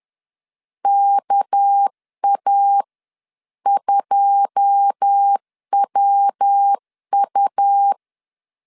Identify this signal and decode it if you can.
Morse code, heard as a keyed tone, about 10 wpm, spaced as for 9 wpm, 800 Hz. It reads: KA2WU
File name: LEZIONI / 9 286 10wpm 2735